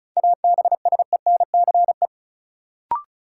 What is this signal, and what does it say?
Morse code, heard as a keyed tone, about 36 wpm, 700 Hz. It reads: ABSENCE
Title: Sets of 4 Words - New Word Drill - 35wpm